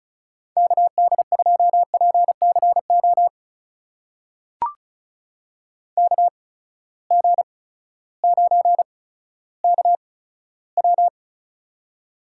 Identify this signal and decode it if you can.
Morse code, heard as a keyed tone, about 36 wpm, spaced as for 16 wpm, 700 Hz. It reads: KD2PCO KG9KW
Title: USA Call Signs - 8x Character Spacing - 35wpm